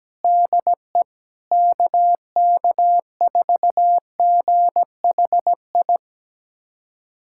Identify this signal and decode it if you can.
Morse code, heard as a keyed tone, about 17 wpm, 700 Hz. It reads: DE KK4GHI
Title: USA Callsign DE Callsign 17wpm